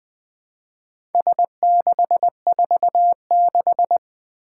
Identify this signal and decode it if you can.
Morse code, heard as a keyed tone, about 20 wpm, 700 Hz. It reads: S646
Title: Four Challenging Letter-Number Combinations 20wpm